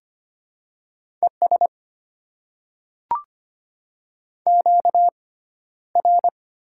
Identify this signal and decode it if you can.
Morse code, heard as a keyed tone, about 25 wpm, spaced as for 13 wpm, 700 Hz. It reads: ES QR
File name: Top 20 QSO Elements - 6x Character Spacing - 25wpm